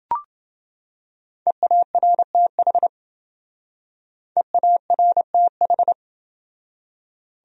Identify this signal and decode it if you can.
Morse code, heard as a keyed tone, about 28 wpm, 700 Hz. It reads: EARTH EARTH